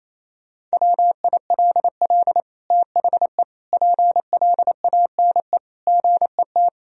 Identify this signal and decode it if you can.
Morse code, heard as a keyed tone, about 28 wpm, 700 Hz. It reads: WILL THE PLANE GET
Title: Sentences from Top 1000 Words - Review Part 3 - 28wpm